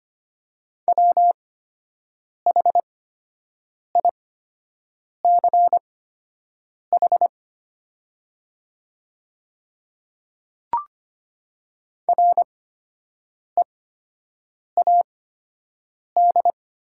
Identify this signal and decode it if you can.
Morse code, heard as a keyed tone, about 25 wpm, spaced as for 7 wpm, 700 Hz. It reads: WHICH READ
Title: 8x Character Spacing - Top 200 Words - 25wpm